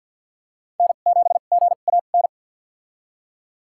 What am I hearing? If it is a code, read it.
Morse code, heard as a keyed tone, about 53 wpm, spaced as for 37 wpm, 700 Hz. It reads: N7GAN
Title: USA Call Signs - 5 Characters - 2x Character Spacing - 50wpm